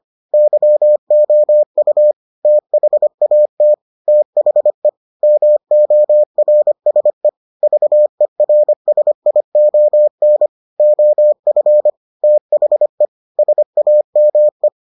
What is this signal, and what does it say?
Morse code, heard as a keyed tone, about 25 wpm, 600 Hz. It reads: YOU THAT THE MORSE VERSION OF THE SAME